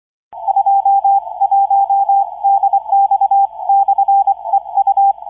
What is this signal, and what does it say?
Morse code, heard as a keyed tone, about 23 wpm, 790 Hz, with radio noise. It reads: 21DX/EU